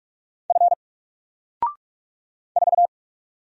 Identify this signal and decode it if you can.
Morse code, heard as a keyed tone, about 44 wpm, 700 Hz. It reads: F 4
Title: Single Letter-Number Alt 45wpm